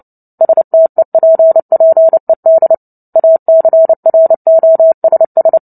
Morse code, heard as a keyed tone, about 30 wpm, 650 Hz. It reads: STEPPED ACROSS
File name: Book Two / 03-Chapter 3 - 29 WPM